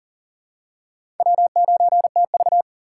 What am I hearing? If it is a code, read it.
Morse code, heard as a keyed tone, about 40 wpm, 700 Hz. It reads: W9TV
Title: USA Call Signs - 4 Characters - 6x Character Spacing - 40wpm